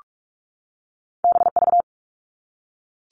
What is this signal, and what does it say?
Morse code, heard as a keyed tone, about 46 wpm, 700 Hz. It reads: BV
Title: All Cut Numbers - 2-Digit 45wpm